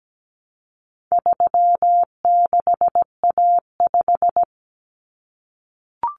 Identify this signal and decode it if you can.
Morse code, heard as a keyed tone, about 17 wpm, 700 Hz. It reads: 36A5
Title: Common Cut Numbers - 4-Digit 17wpm